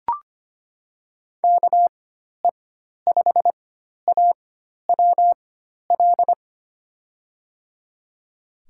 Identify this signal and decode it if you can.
Morse code, heard as a keyed tone, about 25 wpm, spaced as for 12 wpm, 700 Hz. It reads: KE5AWL
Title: USA Call Signs - 6 Characters - 4x Character Spacing - 25wpm